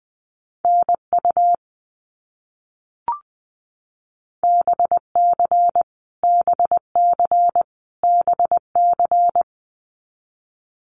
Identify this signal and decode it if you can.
Morse code, heard as a keyed tone, about 20 wpm, 700 Hz. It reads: NU BC BC BC
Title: Canadian Provinces and Territories - Repeated 3x - 20wpm